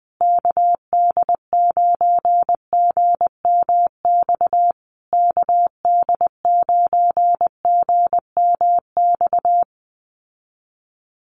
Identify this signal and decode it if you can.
Morse code, heard as a keyed tone, about 20 wpm, 700 Hz. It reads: KD9GMX KD9GMX
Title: USA Call Signs Repeated 3x 20wpm